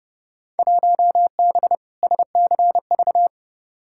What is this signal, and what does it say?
Morse code, heard as a keyed tone, about 30 wpm, 700 Hz. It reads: 1B SCV
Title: Field Day Exchanges Encore - 30wpm